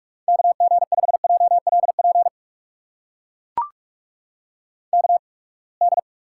Morse code, heard as a keyed tone, about 44 wpm, spaced as for 31 wpm, 700 Hz. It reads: KG5JLP KD
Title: USA Call Signs - 6 Characters - 8x Character Spacing - 45wpm